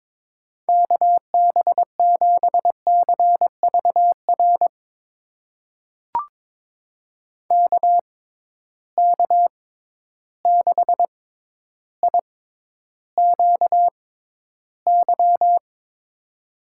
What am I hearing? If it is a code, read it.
Morse code, heard as a keyed tone, about 22 wpm, spaced as for 11 wpm, 700 Hz. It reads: KB7CVR KK6IQY
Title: USA Call Signs - 6 Characters - 6x Character Spacing - 22wpm